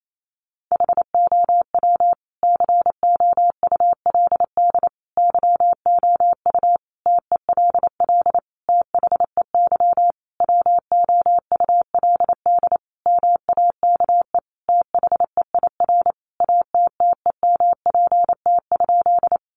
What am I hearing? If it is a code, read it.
Morse code, heard as a keyed tone, about 28 wpm, 700 Hz. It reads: HOW COULD YOU TELL THEY WOULD MAKE THEIR ATTEMPT?
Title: Sentences from Top 1000 Words 28wpm